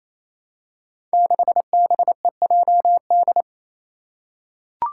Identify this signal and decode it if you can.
Morse code, heard as a keyed tone, about 27 wpm, 700 Hz. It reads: 6BEJD
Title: Five Challenging Letter-Number Combinations 28wpm